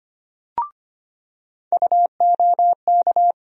Morse code, heard as a keyed tone, about 25 wpm, 700 Hz. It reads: UOK
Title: USA Call Sign Postfixes - 3 Letters 25wpm